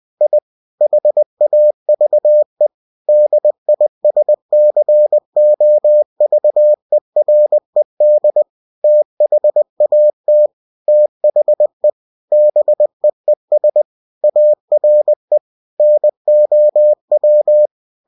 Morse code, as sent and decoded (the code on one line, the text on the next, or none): .. / .... .- ...- . / -.. .. ... -.-. --- ...- . .-. -.. / - .... .- - / - .... / -... . ... / .- .-. / -. --- .--
I HAVE DISCOVERED THAT THE BEES ARE NOW